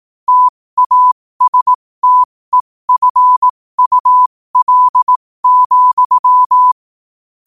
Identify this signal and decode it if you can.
Morse code, heard as a keyed tone, about 18 wpm, spaced as for 15 wpm, 1000 Hz. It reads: TASTEFUL,